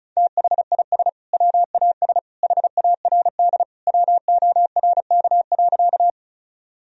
Morse code, heard as a keyed tone, about 36 wpm, 700 Hz. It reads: THIS WAS HARD WORK.
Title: Sentences from Top 1000 Words - Review Part 1 - 35wpm